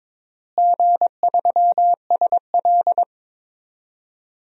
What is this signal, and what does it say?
Morse code, heard as a keyed tone, about 21 wpm, 700 Hz. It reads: G3SL
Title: DXCC-100 Call Signs - 4 Characters - Repeated 3x - 22wpm